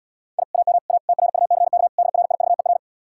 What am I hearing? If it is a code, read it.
Morse code, heard as a keyed tone, about 80 wpm, 700 Hz. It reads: I AM A TRUCK DRIVER